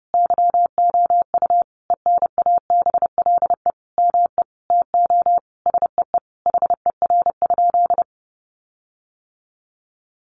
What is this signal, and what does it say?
Morse code, heard as a keyed tone, about 30 wpm, 700 Hz. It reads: YOU ENABLE ME TO SEE HER?
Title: Sentences from Top 1500 Words 30wpm